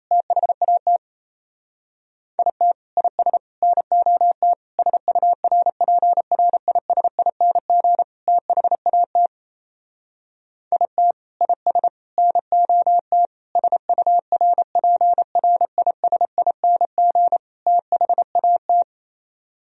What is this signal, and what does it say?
Morse code, heard as a keyed tone, about 32 wpm, 700 Hz. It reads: THAT IT IS NOT SURPRISING THAT IT IS NOT SURPRISING THAT
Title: Speed-Racing - Sets of 5 Words 28wpm